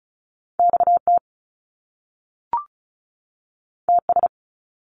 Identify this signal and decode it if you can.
Morse code, heard as a keyed tone, about 36 wpm, 700 Hz. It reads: XT TS
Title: Top Two Letter Combinations in Words 35wpm